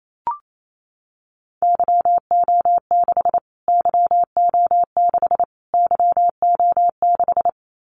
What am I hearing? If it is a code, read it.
Morse code, heard as a keyed tone, about 28 wpm, 700 Hz. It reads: YO6 YO6 YO6
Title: Callsign Prefixes for Top 25 DXCC - Repeated 3x - 28wpm